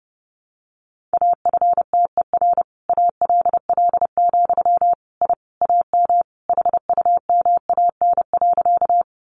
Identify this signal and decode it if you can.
Morse code, heard as a keyed tone, about 28 wpm, 700 Hz. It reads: AFTER ALL, I AM HUMAN.
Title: Sentences from Top 600 Words 30wpm